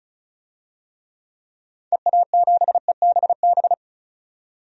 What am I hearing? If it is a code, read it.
Morse code, heard as a keyed tone, about 34 wpm, 700 Hz. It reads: EA7EBB
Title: DXCC-100 Call Signs - 6 Characters - 35wpm